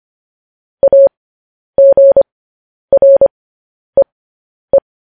Morse code, heard as a keyed tone, about 25 wpm, spaced as for 10 wpm, 550 Hz. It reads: AGREE
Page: news headlines in morse code at 10 wpm